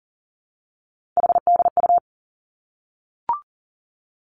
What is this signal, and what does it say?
Morse code, heard as a keyed tone, about 40 wpm, 700 Hz. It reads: HDU
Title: Three Challenging Letter-Number Combinations 40wpm